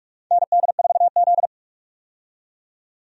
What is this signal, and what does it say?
Morse code, heard as a keyed tone, about 41 wpm, 700 Hz. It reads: ND4Z
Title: USA Call Signs - 4 Characters - 45wpm